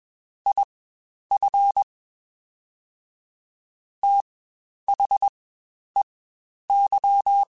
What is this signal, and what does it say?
Morse code, heard as a keyed tone, about 21 wpm, spaced as for 10 wpm, 800 Hz. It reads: IF THEY
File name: Quotes Spaced